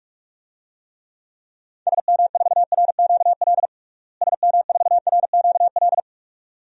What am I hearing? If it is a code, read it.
Morse code, heard as a keyed tone, about 46 wpm, 700 Hz. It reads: SM4RQL SM4RQL
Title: DXCC-100 Call Signs - 6 Characters - Repeated 2x - 45wpm